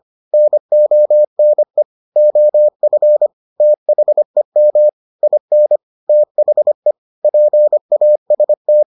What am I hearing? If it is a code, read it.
Morse code, heard as a keyed tone, about 25 wpm, 600 Hz. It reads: NONE OF THEM IN THE PAST